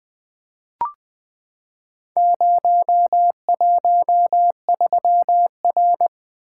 Morse code, heard as a keyed tone, about 20 wpm, 700 Hz. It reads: 013R
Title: Four Letter-Number Combinations - 20wpm